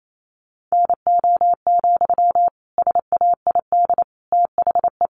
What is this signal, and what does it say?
Morse code, heard as a keyed tone, about 28 wpm, 700 Hz. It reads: NO, SAID THE
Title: Sentences from Top 1500 Words 28wpm